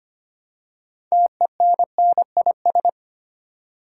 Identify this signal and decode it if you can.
Morse code, heard as a keyed tone, about 25 wpm, 700 Hz. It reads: TENNIS
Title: Six Letter Words - 25wpm